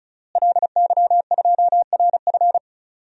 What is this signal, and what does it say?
Morse code, heard as a keyed tone, about 34 wpm, 700 Hz. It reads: LY2RF
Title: DXCC-100 Call Signs - 5 Characters - Repeated 2x - 35wpm